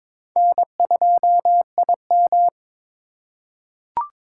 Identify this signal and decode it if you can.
Morse code, heard as a keyed tone, about 22 wpm, 700 Hz. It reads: N2IM
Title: USA Call Signs - 3x Character Spacing - 22wpm